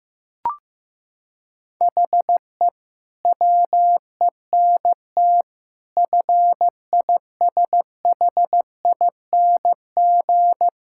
Morse code, heard as a keyed tone, about 15 wpm, 700 Hz. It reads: HE WENT FISHING
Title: Sentences from Top 1000 Words - Review Part 5 - 15wpm